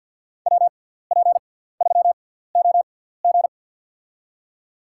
Morse code, heard as a keyed tone, about 48 wpm, spaced as for 17 wpm, 690 Hz. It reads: WP3OG
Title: USA Call Signs - 6x Character Spacing - 50wpm